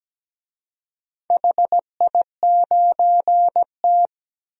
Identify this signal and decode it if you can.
Morse code, heard as a keyed tone, about 17 wpm, 700 Hz. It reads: HI9T
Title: DXCC-100 Call Signs - 4 Characters - 17wpm